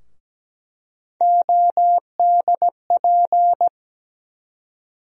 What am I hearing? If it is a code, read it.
Morse code, heard as a keyed tone, about 17 wpm, 700 Hz. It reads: ODP
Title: USA Call Sign Postfixes - 3 Letters 17wpm